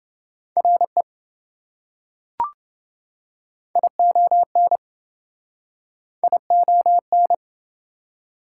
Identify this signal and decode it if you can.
Morse code, heard as a keyed tone, about 28 wpm, 700 Hz. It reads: RE ION ION